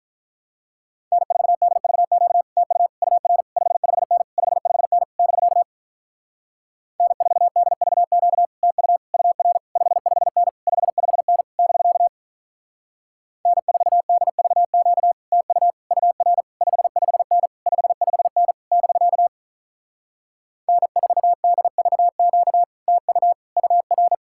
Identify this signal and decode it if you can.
Morse code, heard as a keyed tone, about 46 wpm, 700 Hz. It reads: N4DVQ TU UR 55N 55N <BK> N4DVQ TU UR 55N 55N <BK> N4DVQ TU UR 55N 55N <BK> N4DVQ TU UR